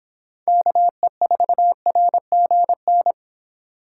Morse code, heard as a keyed tone, about 26 wpm, 700 Hz. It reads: KE4RGN